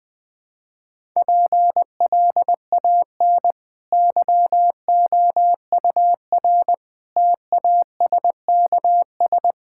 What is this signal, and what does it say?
Morse code, heard as a keyed tone, about 20 wpm, 700 Hz. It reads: PLAN YOUR TASKS
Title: Easy Sentences with Hints - 20wpm